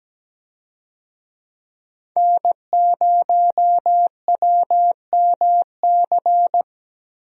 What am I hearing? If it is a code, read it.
Morse code, heard as a keyed tone, about 17 wpm, 700 Hz. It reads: N0WMC
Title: USA Call Signs - 5 Characters - 17wpm